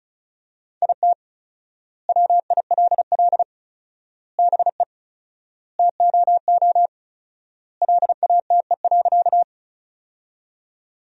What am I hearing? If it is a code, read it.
Morse code, heard as a keyed tone, about 36 wpm, 700 Hz. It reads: IT WILL BE TOO LATE.
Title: 4x Word-Spacing - Sentences from Top 300 Words - 35wpm